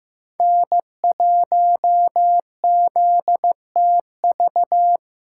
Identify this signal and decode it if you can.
Morse code, heard as a keyed tone, about 15 wpm, 700 Hz. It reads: N1ZTV
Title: USA Callsign DE Callsign 15wpm